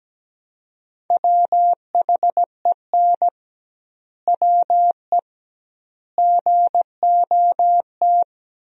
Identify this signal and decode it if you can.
Morse code, heard as a keyed tone, about 17 wpm, 700 Hz. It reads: WHEN WE GOT
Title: 2x Word-Spacing - Sentences from Top 500 Words - 17wpm